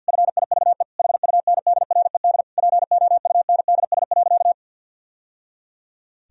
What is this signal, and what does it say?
Morse code, heard as a keyed tone, about 51 wpm, 700 Hz. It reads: FIVE HUNDRED POUNDS.
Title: Sentences from Top 1000 Words - Review Part 4 - 50wpm